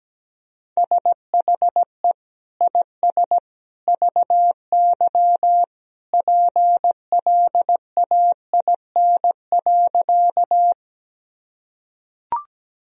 Morse code, heard as a keyed tone, about 17 wpm, 700 Hz. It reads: SHE IS VY PLAIN.
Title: Sentences from Top 400 Words and Abbreviations 17wpm